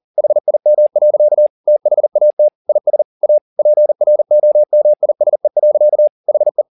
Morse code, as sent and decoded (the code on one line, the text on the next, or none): .... .. -- .-.-.- / - .... .- - / .. ... / .- / .--. .-. --- -- .. ... .-.-.- / .... .
HIM. THAT IS A PROMISE. HE